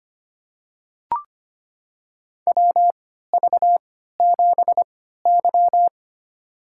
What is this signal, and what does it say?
Morse code, heard as a keyed tone, about 25 wpm, spaced as for 14 wpm, 700 Hz. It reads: WV7Y
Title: USA Call Signs - 3x Character Spacing - 25wpm